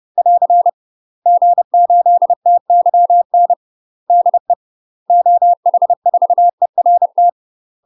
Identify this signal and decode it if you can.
Morse code, heard as a keyed tone, about 30 wpm, 700 Hz. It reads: <AR> G8TYN DE OH4ERT
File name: UK MASTER QSOs TRAINING / 30WPM / UK MASTER QSOs 0034